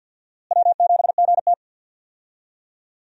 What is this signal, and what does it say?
Morse code, heard as a keyed tone, about 47 wpm, 700 Hz. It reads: W7GT
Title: USA Call Signs - 4 Characters - Repeated 3x - 50wpm